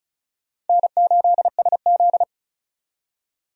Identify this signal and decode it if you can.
Morse code, heard as a keyed tone, about 34 wpm, 700 Hz. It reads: N8SZ
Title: USA Call Signs - 4 Characters - 35wpm